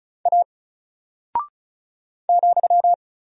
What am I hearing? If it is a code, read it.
Morse code, heard as a keyed tone, about 36 wpm, 700 Hz. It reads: A ,
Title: Single Character (Letter:Number:Punctuation:Prosign) Alt 35wpm